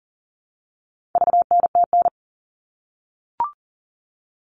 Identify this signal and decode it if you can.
Morse code, heard as a keyed tone, about 40 wpm, 700 Hz. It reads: VNTN
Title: All Cut Numbers - 4-Digit 40wpm